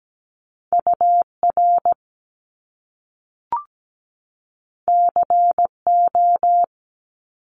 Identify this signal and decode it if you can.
Morse code, heard as a keyed tone, about 17 wpm, 700 Hz. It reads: UR CO
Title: Top Two Letter Combinations in Words 17wpm